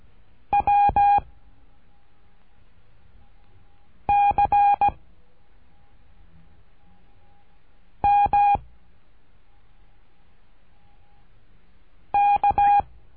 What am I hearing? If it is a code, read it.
Morse code, heard as a keyed tone, about 17 wpm, 810 Hz. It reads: W C M K
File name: cursos / CURSO DE TELEGRAFIA / Telegrafia - Trilha 13